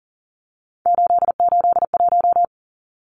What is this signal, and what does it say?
Morse code, heard as a keyed tone, about 40 wpm, 700 Hz. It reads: 881
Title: Numbers - 3-Digit 40wpm